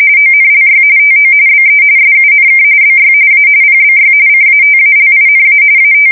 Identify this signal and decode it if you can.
RTTY, about 45.45 baud. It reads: ONLY TO BE SUNK BY A PBY CATALINA AT
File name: fsk170 / USS-Massachusetts